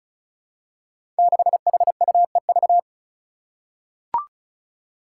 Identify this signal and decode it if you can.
Morse code, heard as a keyed tone, about 34 wpm, 700 Hz. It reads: 6HUEV
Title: Five Challenging Letter-Number Combinations 35wpm